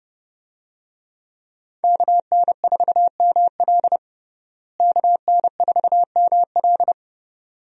Morse code, heard as a keyed tone, about 30 wpm, 700 Hz. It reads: KN4ML KN4ML
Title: USA Call Signs - 5 Characters - Repeated 2x - 30wpm